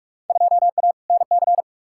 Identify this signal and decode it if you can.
Morse code, heard as a keyed tone, about 48 wpm, 700 Hz. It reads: 2A NC
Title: Field Day Exchanges - 45wpm